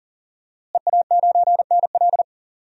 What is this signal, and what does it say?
Morse code, heard as a keyed tone, about 40 wpm, 700 Hz. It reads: EA9NL
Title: DXCC-100 Call Signs - 5 Characters - Repeated 3x - 40wpm